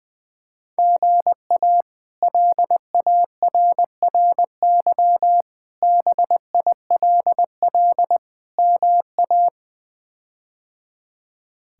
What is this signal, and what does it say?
Morse code, heard as a keyed tone, about 21 wpm, 700 Hz. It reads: GA LARRY BILL MA
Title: K1USN SST QSO Practice - 20wpm